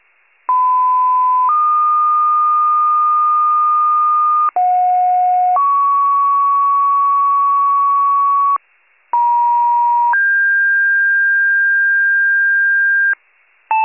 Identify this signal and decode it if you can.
Two-tone page, heard as a one-second tone, then a three-second tone, about 1010 then 1220 Hz, about 730 then 1120 Hz, about 950 then 1600 Hz. It is heard from 0.5 to 13.1 s.